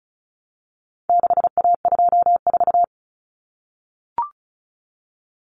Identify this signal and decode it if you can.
Morse code, heard as a keyed tone, about 34 wpm, 700 Hz. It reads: 6A24
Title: Common Cut Numbers - 4-Digit 35wpm